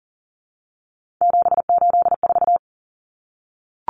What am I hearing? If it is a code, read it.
Morse code, heard as a keyed tone, about 40 wpm, 700 Hz. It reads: 784
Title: Numbers - 3-Digit 40wpm